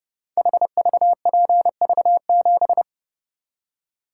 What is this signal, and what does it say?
Morse code, heard as a keyed tone, about 32 wpm, 700 Hz. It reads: HVPV7
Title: Code Groups - 5 Letter:Numbers - 30wpm